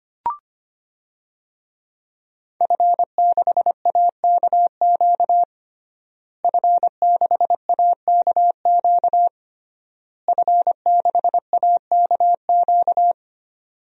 Morse code, heard as a keyed tone, about 25 wpm, 700 Hz. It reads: F6AKQ F6AKQ F6AKQ
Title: DXCC-100 Call Signs - 5 Characters - Repeated 3x - 25wpm